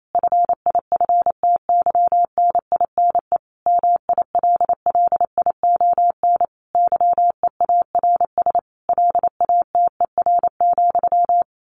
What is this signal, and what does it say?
Morse code, heard as a keyed tone, about 28 wpm, 700 Hz. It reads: FIFTYNINE MILLION YEARS LATER,
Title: A Sound Of Thunder 28wpm